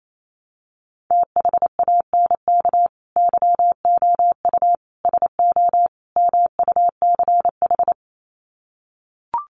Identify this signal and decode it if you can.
Morse code, heard as a keyed tone, about 28 wpm, 700 Hz. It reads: THANK YOU SO MUCH
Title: Sets of 4 Words 28wpm